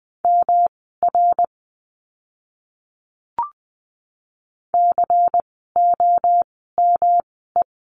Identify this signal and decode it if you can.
Morse code, heard as a keyed tone, about 20 wpm, spaced as for 15 wpm, 700 Hz. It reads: MR COME
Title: 244 Lesson 39f - Letter z - Words 20wpm